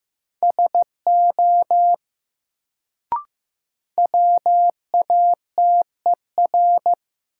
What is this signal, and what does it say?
Morse code, heard as a keyed tone, about 15 wpm, 700 Hz. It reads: SO WATER
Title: Sentences from Top 100 Words - New Word Drill - 15wpm